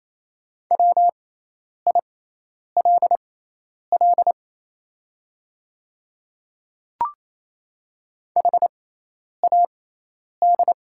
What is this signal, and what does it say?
Morse code, heard as a keyed tone, about 28 wpm, spaced as for 10 wpm, 690 Hz. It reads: WILL HAD
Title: 6x Character Spacing - Top 100 Words - 28wpm